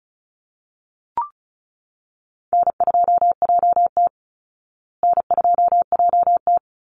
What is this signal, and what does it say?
Morse code, heard as a keyed tone, about 34 wpm, 700 Hz. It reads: N2JT N2JT
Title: Easy Call Signs Repeated 2x - 35wpm